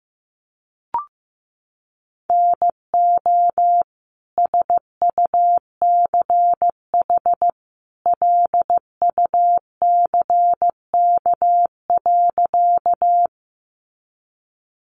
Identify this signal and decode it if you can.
Morse code, heard as a keyed tone, about 15 wpm, 700 Hz. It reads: NO SUCH LUCK.
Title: Sentences from Top 1500 Words 15wpm